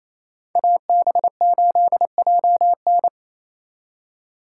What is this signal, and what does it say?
Morse code, heard as a keyed tone, about 33 wpm, 700 Hz. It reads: AB8JN
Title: Speed Racing - USA 2x2 Call Signs 28wpm